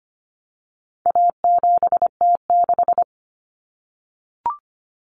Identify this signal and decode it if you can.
Morse code, heard as a keyed tone, about 25 wpm, 700 Hz. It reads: A7T6
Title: Common Cut Numbers - 4-Digit 25wpm